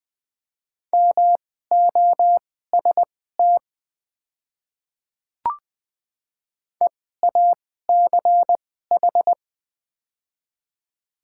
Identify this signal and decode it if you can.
Morse code, heard as a keyed tone, about 20 wpm, spaced as for 14 wpm, 690 Hz. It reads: MOST EACH
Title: 2x Character Spacing - Top 300 Words - 20wpm